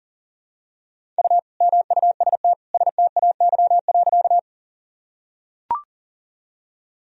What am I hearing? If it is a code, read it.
Morse code, heard as a keyed tone, about 37 wpm, 700 Hz. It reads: U MUST STAY.
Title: Sentences from Top 500 Words and Abbreviations 40wpm